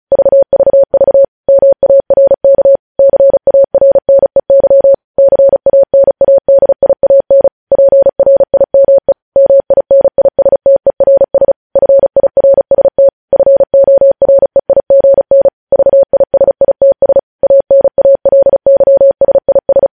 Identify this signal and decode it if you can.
Morse code, heard as a keyed tone, about 36 wpm, 550 Hz. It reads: VVV MARK CARNEY CANADIAN PRIME MINISTERS FIRST FOREIGN VISITS ANALYSIS
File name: podcast / News headlines-1x-35x35